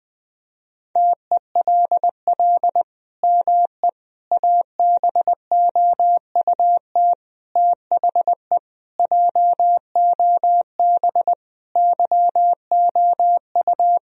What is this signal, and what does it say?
Morse code, heard as a keyed tone, about 20 wpm, 700 Hz. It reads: TELL ME ABOUT THE JOB YOU
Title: Sentences from Top 1000 Words - Review Part 3 - 20wpm